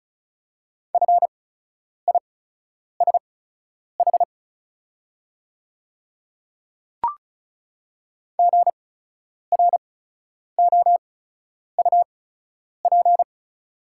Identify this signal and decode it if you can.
Morse code, heard as a keyed tone, about 36 wpm, spaced as for 10 wpm, 700 Hz. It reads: FISH GROUP
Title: 8x Character Spacing - Top 300 Words - 35wpm